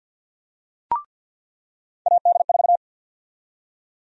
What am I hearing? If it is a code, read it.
Morse code, heard as a keyed tone, about 52 wpm, 700 Hz. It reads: AD4
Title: USA Call Sign Prefixes - 50wpm